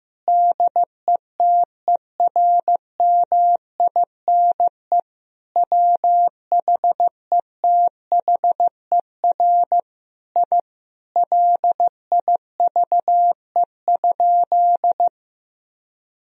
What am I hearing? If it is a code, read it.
Morse code, heard as a keyed tone, about 15 wpm, 700 Hz. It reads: DETERMINE WHETHER I LIVE?
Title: Sentences from Top 1000 Words - Review Part 4 - 15wpm